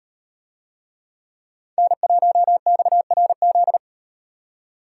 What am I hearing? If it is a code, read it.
Morse code, heard as a keyed tone, about 35 wpm, 700 Hz. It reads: N1XRZ